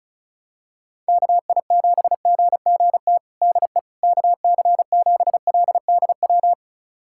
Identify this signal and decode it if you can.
Morse code, heard as a keyed tone, about 34 wpm, 700 Hz. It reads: KI7GGT DE KC7LDW
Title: USA Callsign DE Callsign - 6 Characters - 35wpm